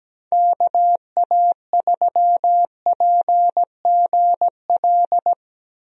Morse code, heard as a keyed tone, about 17 wpm, 700 Hz. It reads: KA3PGL
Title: USA Call Signs - 6 Characters - 17wpm